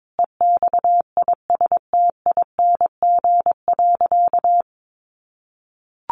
Morse code, heard as a keyed tone, about 22 wpm, 700 Hz. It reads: EXISTING.
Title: Sentences from Top 1500 Words 22wpm